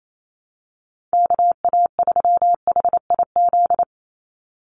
Morse code, heard as a keyed tone, about 28 wpm, 700 Hz. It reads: KA3HIZ
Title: USA Call Signs 28wpm